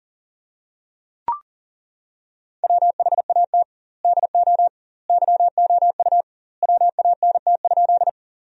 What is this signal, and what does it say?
Morse code, heard as a keyed tone, about 40 wpm, 700 Hz. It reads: WHAT DO YOU WANT?
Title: 2x Word-Spacing - Sentences from Top 200 Words - 40wpm